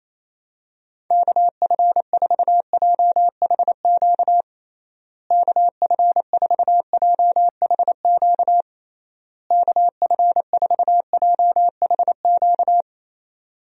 Morse code, heard as a keyed tone, about 28 wpm, 700 Hz. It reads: KF4JHQ KF4JHQ KF4JHQ
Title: USA Call Signs - 6 Characters - Repeated 3x - 28wpm